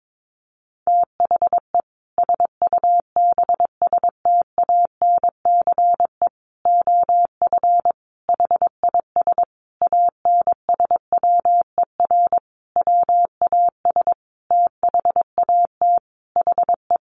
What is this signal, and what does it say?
Morse code, heard as a keyed tone, about 22 wpm, 700 Hz. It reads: THE SUBSTANCE OF HIS ANSWER WAS THAT HE
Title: Sentences from Top 1500 Words 22wpm